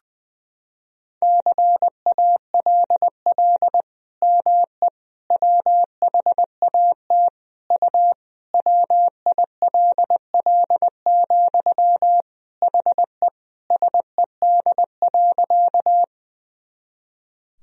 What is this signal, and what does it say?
Morse code, heard as a keyed tone, about 20 wpm, 700 Hz. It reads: CALL ME WHAT U WILL, HE SED.
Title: Sentences from Top 200 Words and Abbreviations 20wpm